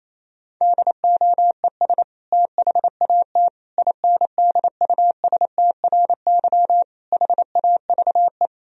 Morse code, heard as a keyed tone, about 28 wpm, 700 Hz. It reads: DOES THAT INDUSTRY HAVE
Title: Sentences from Top 1000 Words - Review Part 4 - 28wpm